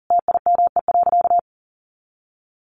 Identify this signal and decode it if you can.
Morse code, heard as a keyed tone, about 40 wpm, 700 Hz. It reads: TIME.
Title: Sentences from Top 600 Words 40wpm